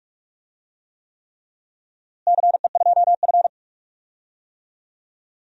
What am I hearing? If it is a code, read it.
Morse code, heard as a keyed tone, about 41 wpm, 700 Hz. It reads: CE2F
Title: DXCC-100 Call Signs - 4 Characters - Repeated 3x - 45wpm